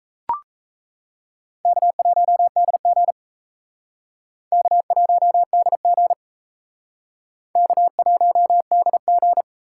Morse code, heard as a keyed tone, about 35 wpm, 700 Hz. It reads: K1DG K1DG K1DG
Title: Speed Racing - USA 1x2 Call Signs 28wpm